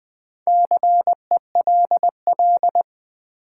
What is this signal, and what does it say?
Morse code, heard as a keyed tone, about 20 wpm, 700 Hz. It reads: CELL
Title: Sentences from Top 1100 Words - New Word Drill Alt - 20wpm